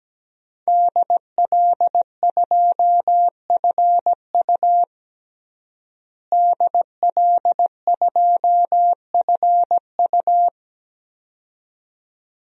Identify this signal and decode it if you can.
Morse code, heard as a keyed tone, about 17 wpm, 700 Hz. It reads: DL2FU DL2FU
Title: DXCC-100 Call Signs - 5 Characters - Repeated 2x - 17wpm